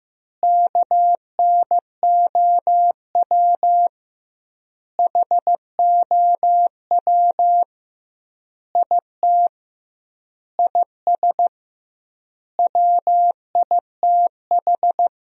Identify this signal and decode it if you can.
Morse code, heard as a keyed tone, about 15 wpm, 700 Hz. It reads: KNOW HOW IT IS WITH
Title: 2x Word-Spacing - Sentences from Top 100 Words - 15wpm